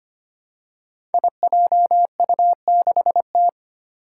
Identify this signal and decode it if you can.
Morse code, heard as a keyed tone, about 25 wpm, 700 Hz. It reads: IJU6T
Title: Five Challenging Letter-Number Combinations 25wpm